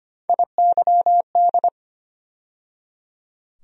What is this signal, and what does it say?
Morse code, heard as a keyed tone, about 25 wpm, 700 Hz. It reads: IYD
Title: USA Call Sign Postfixes - 3 Letters 25wpm